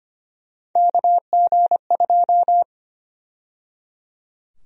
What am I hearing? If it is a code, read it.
Morse code, heard as a keyed tone, about 25 wpm, 700 Hz. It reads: KG2
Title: USA Call Sign Prefixes - 25wpm